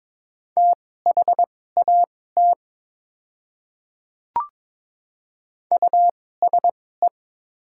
Morse code, heard as a keyed tone, about 21 wpm, spaced as for 16 wpm, 700 Hz. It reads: THAT USE
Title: 2x Character Spacing - Top 100 Words - 22wpm